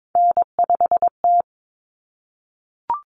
Morse code, heard as a keyed tone, about 22 wpm, 700 Hz. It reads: N5T
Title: Common Cut Numbers - 3-Digit 22wpm